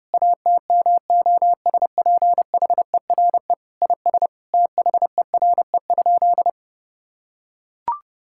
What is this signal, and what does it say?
Morse code, heard as a keyed tone, about 30 wpm, 700 Hz. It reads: ATMOSPHERE IS THERE?
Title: Sentences from Top 1600 Words 30wpm